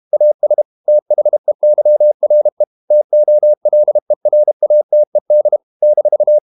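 Morse code, heard as a keyed tone, about 32 wpm, 600 Hz. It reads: AS THEYRE TOLERATED -
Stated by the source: Morse code practice at 32 wpm from CW med Gnister, reading Ken Follett's novel Never in English.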